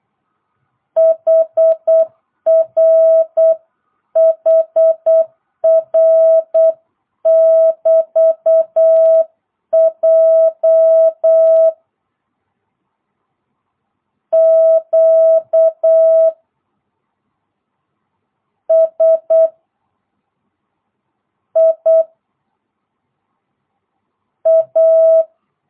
Morse code, heard as a keyed tone, about 8 wpm, 650 Hz. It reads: HR HR <BT> J Q S I A